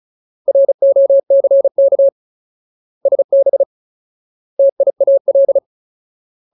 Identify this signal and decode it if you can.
Morse code, heard as a keyed tone, about 34 wpm, 550 Hz. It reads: ROCK SB TIAL
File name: Ultimate Word List / 35WPM Ultimate